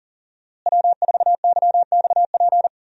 Morse code, heard as a keyed tone, about 40 wpm, 700 Hz. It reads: W4YXP
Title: USA Call Signs - 5 Characters - Repeated 3x - 40wpm